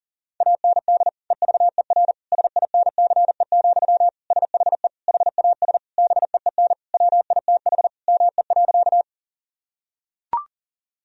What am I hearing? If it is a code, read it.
Morse code, heard as a keyed tone, about 37 wpm, 700 Hz. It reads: AND EVER SINCE, SHE HAS BEEN WITH ME.
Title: Sentences from Top 1000 Words - Review Part 1 - 40wpm